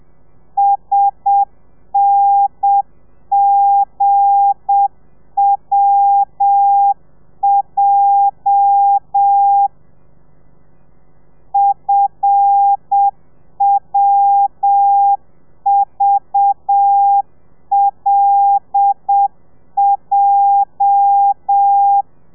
Morse code, heard as a keyed tone, about 7 wpm, 800 Hz. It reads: SNGWJ FWVLJ
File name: telegrafia / i2tfj / Lez8 35